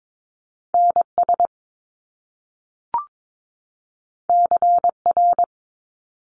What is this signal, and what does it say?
Morse code, heard as a keyed tone, about 22 wpm, 700 Hz. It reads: NS CR
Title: Top Two Letter Combinations in Words 22wpm